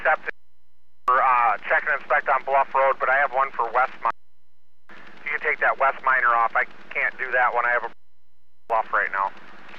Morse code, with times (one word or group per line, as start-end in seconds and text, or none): none